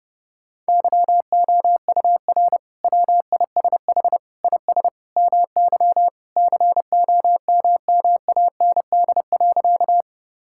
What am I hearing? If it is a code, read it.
Morse code, heard as a keyed tone, about 30 wpm, 700 Hz. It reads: YOUR WISH IS MY COMMAND.
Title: Sentences from Top 1600 Words 30wpm